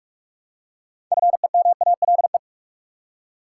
Morse code, heard as a keyed tone, about 46 wpm, 700 Hz. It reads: FEMALE
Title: Six Letter Words - 45wpm